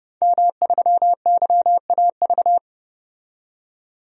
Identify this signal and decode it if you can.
Morse code, heard as a keyed tone, about 30 wpm, 700 Hz. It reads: M3YAV
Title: DXCC-100 Call Signs - 5 Characters - 30wpm